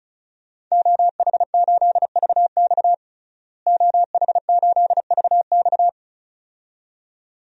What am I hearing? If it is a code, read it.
Morse code, heard as a keyed tone, about 34 wpm, 700 Hz. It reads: OH8VX OH8VX
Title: DXCC-100 Call Signs - 5 Characters - Repeated 2x - 35wpm